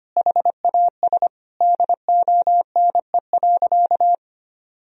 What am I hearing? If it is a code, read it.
Morse code, heard as a keyed tone, about 25 wpm, 700 Hz. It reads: HAS DONE.
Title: Sentences from Top 1000 Words - Review Part 1 - 25wpm